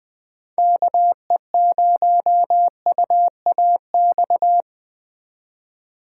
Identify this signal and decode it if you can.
Morse code, heard as a keyed tone, about 21 wpm, 700 Hz. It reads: KE0UAX